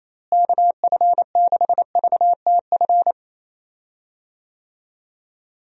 Morse code, heard as a keyed tone, about 28 wpm, 700 Hz. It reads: KF6VTF
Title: USA Call Signs - 6 Characters - Repeated 3x - 28wpm